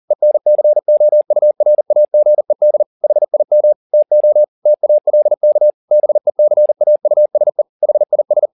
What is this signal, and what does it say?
Morse code, heard as a keyed tone, about 40 wpm, 600 Hz. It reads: ENCOURAGED HIM TO TALK BECAUSE HIS